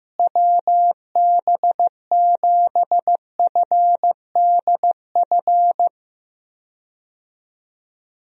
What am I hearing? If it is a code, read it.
Morse code, heard as a keyed tone, about 15 wpm, 700 Hz. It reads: WB7FDF
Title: USA Call Signs - 6 Characters - Repeated 2x - 15wpm